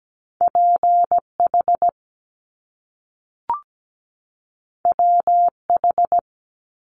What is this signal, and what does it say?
Morse code, heard as a keyed tone, about 17 wpm, 700 Hz. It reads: PH WH
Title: Top Two Letter Combinations in Words 17wpm